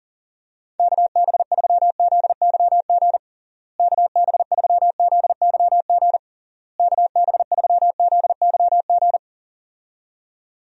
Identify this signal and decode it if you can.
Morse code, heard as a keyed tone, about 40 wpm, 700 Hz. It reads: KB3ZYG KB3ZYG KB3ZYG
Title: USA Call Signs - 6 Characters - Repeated 3x - 40wpm